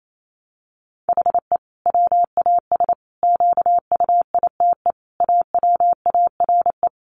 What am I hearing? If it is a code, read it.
Morse code, heard as a keyed tone, about 28 wpm, 700 Hz. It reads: HE WAS QUITE AWARE
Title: Sentences from Top 1400 Words 28wpm